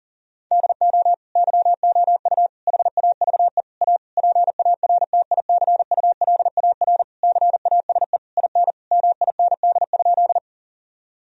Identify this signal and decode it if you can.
Morse code, heard as a keyed tone, about 40 wpm, 700 Hz. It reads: DO YOU HAVE A PARTICULAR CASE IN MIND?
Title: Sentences from Top 1000 Words - Review Part 4 - 40wpm